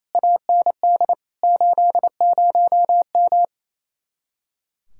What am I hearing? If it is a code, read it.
Morse code, heard as a keyed tone, about 27 wpm, 700 Hz. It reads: AND 80M
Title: Speed-Racing Common QSO Phrases - 28wpm